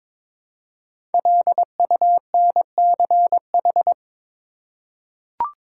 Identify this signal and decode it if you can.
Morse code, heard as a keyed tone, about 22 wpm, 700 Hz. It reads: LUNCH
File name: 5 Letter Words - First 3 Letters Given - 22wpm